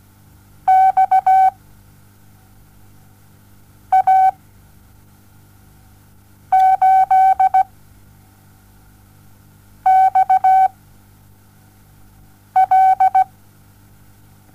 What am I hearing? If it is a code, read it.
Morse code, heard as a keyed tone, about 17 wpm, spaced as for 4 wpm, 760 Hz. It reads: XA8XL
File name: Onlinelehrgang / morsen / DJ4UF Morsen07